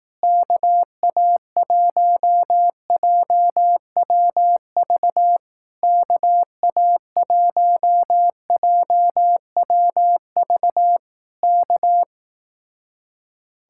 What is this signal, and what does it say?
Morse code, heard as a keyed tone, about 18 wpm, 700 Hz. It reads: KA1JWV KA1JWV K
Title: Speed-Racing Common QSO Phrases Encore - 15wpm